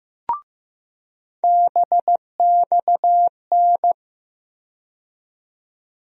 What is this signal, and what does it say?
Morse code, heard as a keyed tone, about 15 wpm, 700 Hz. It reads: BXN
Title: USA Call Sign Postfixes - 3 Letters 15wpm